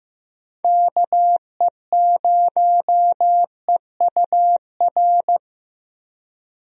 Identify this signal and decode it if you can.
Morse code, heard as a keyed tone, about 15 wpm, 700 Hz. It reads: KE0EUR
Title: USA Call Signs - 6 Characters - Repeated 2x - 15wpm